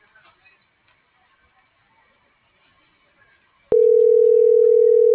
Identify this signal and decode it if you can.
Ringback tone (North America), 440 + 480 Hz.
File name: CENTRAL SAMSUNG SEM / out marisol.sem